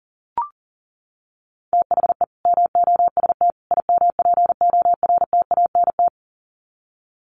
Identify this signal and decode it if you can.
Morse code, heard as a keyed tone, about 37 wpm, 700 Hz. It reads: THE MOST IMPORTANT
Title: Sets of 3 Words 40wpm